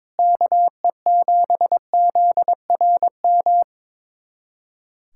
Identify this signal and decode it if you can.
Morse code, heard as a keyed tone, about 22 wpm, 700 Hz. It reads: KE7ZRM
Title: USA Callsign DE Callsign 22wpm